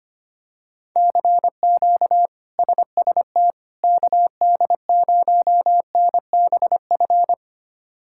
Morse code, heard as a keyed tone, about 25 wpm, 700 Hz. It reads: CQ SST KD0NBF
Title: K1USN SST QSO Practice - 25wpm